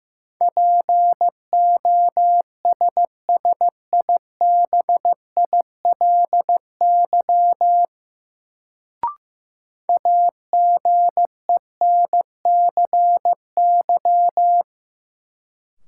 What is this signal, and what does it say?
Morse code, heard as a keyed tone, about 15 wpm, 700 Hz. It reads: POSSIBILY AGENCY
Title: Sentences from Top 1300 Words - New Word Drill - 15wpm